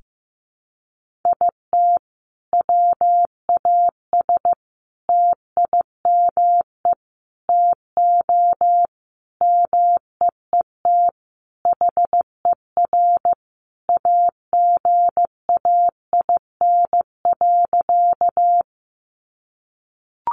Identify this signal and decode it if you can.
Morse code, heard as a keyed tone, about 15 wpm, 700 Hz. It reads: IT WAS TIME TO MEET HER AGAIN.
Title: Sentences from Top 1100 Words 15wpm